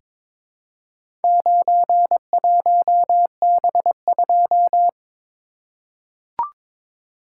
Morse code, heard as a keyed tone, about 22 wpm, 700 Hz. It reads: 91B2
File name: practice / 22wpm / Four Letter-Number Combinations - 22wpm